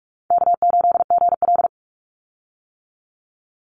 Morse code, heard as a keyed tone, about 44 wpm, 700 Hz. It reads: K8GL
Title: Call Signs - 45wpm